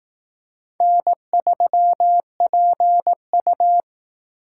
Morse code, heard as a keyed tone, about 18 wpm, 700 Hz. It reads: N3PU
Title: Speed Racing - USA 1x2 Call Signs 15wpm